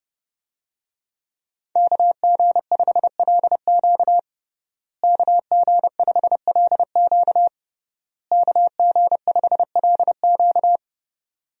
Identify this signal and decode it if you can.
Morse code, heard as a keyed tone, about 30 wpm, 700 Hz. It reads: KG5LQ KG5LQ KG5LQ
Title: USA Call Signs - 5 Characters - Repeated 3x - 30wpm